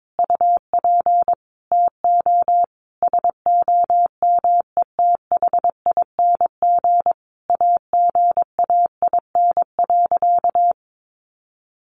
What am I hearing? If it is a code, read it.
Morse code, heard as a keyed tone, about 22 wpm, 700 Hz. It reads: UP TO SOMETHING AGAIN.
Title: Sentences from Top 1100 Words 22wpm